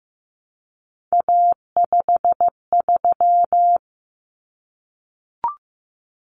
Morse code, heard as a keyed tone, about 15 wpm, 700 Hz. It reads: A53
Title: Common Cut Numbers - 3-Digit 15wpm